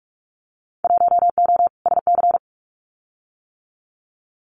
Morse code, heard as a keyed tone, about 44 wpm, 700 Hz. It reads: 1O SC
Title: Winter Field Day Exchanges 45wpm